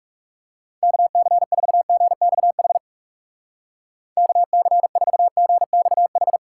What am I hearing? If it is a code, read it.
Morse code, heard as a keyed tone, about 38 wpm, 700 Hz. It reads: KC4GXH KC4GXH